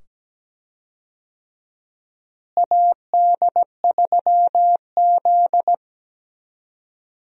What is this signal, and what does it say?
Morse code, heard as a keyed tone, about 17 wpm, 700 Hz. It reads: AD3Z
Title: USA Call Signs - 4 Characters - Repeated 2x - 17wpm